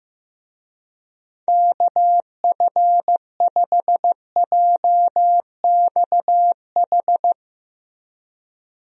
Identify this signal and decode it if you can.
Morse code, heard as a keyed tone, about 15 wpm, 700 Hz. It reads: KF5JXH
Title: USA Call Signs - 6 Characters - Repeated 2x - 15wpm